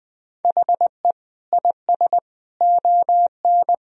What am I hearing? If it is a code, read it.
Morse code, heard as a keyed tone, about 20 wpm, 700 Hz. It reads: HE IS ON
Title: Sentences from Top 1000 Words - Review Part 4 - 20wpm